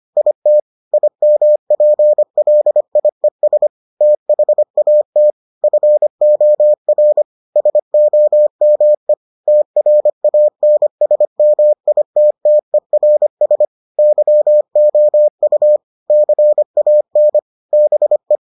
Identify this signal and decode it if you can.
Morse code, heard as a keyed tone, about 25 wpm, 600 Hz. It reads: IT IMPLIES THAT FOR SOME TRANSMITTERS YOU CAN BE